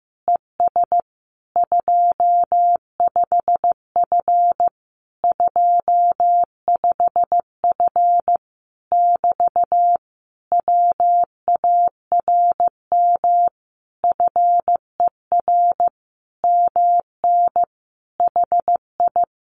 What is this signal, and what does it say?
Morse code, heard as a keyed tone, about 15 wpm, 700 Hz. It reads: ES 25F 25F <BT> WARM FER MN HI
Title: Common QSO Phrases Encore 15wpm